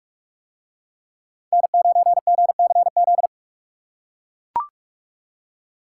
Morse code, heard as a keyed tone, about 43 wpm, 700 Hz. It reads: N9GCZ